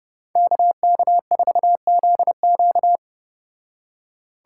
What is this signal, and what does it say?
Morse code, heard as a keyed tone, about 29 wpm, 700 Hz. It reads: KK4ZQ